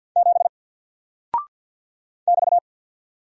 Morse code, heard as a keyed tone, about 50 wpm, 700 Hz. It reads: 7 <BT>
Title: Single Character (Letter:Number:Punctuation:Prosign) 50wpm